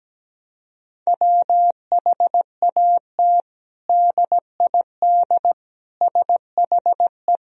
Morse code, heard as a keyed tone, about 17 wpm, 700 Hz. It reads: WHAT DID SHE
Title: Sentences from Top 1000 Words - Review Part 2 - 17wpm